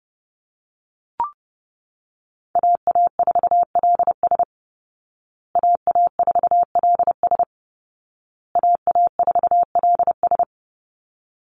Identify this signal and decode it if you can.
Morse code, heard as a keyed tone, about 30 wpm, 700 Hz. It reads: AA4LS AA4LS AA4LS
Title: Call Signs Repeated 2x - 30wpm